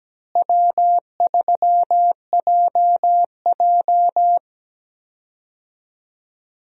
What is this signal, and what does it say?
Morse code, heard as a keyed tone, about 17 wpm, 700 Hz. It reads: W3JJ
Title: USA Call Signs - 4 Characters - Repeated 3x - 17wpm